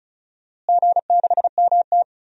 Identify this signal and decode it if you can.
Morse code, heard as a keyed tone, about 36 wpm, 700 Hz. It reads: G6MT
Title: DXCC-100 Call Signs - 4 Characters - Repeated 3x - 35wpm